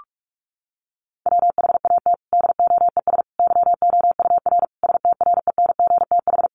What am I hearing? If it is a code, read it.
Morse code, heard as a keyed tone, about 44 wpm, 700 Hz. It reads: WHAT DOES YOUR STRENGTH
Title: Sentences from Top 1400 Words 45wpm